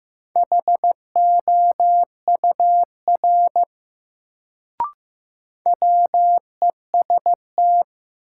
Morse code, heard as a keyed tone, about 15 wpm, 700 Hz. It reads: HOUR WEST
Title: Sets of 5 Words - New Word Drill - 15wpm